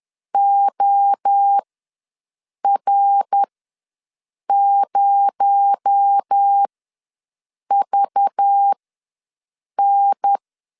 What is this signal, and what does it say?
Morse code, heard as a keyed tone, about 10 wpm, spaced as for 6 wpm, 800 Hz. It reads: OR0VN
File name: LEZIONI / 8 YJ45 / 8 Y ws 10wpm 3952